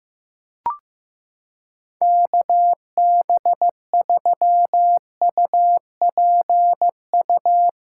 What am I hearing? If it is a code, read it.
Morse code, heard as a keyed tone, about 15 wpm, 700 Hz. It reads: KB3UPU
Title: USA Callsign DE Callsign 15wpm